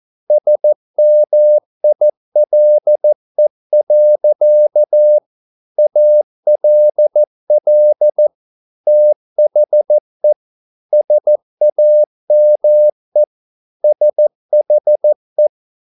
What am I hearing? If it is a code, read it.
Morse code, heard as a keyed tone, about 14 wpm, 600 Hz. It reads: SMILE. ALL THE SAME SHE